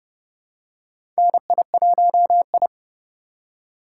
Morse code, heard as a keyed tone, about 30 wpm, 700 Hz. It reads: NI1I
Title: USA Call Signs - 4 Characters - 6x Character Spacing - 30wpm